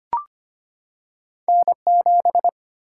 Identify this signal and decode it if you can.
Morse code, heard as a keyed tone, about 25 wpm, 700 Hz. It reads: N 7
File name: Two Letter-Number Combinations - 25wpm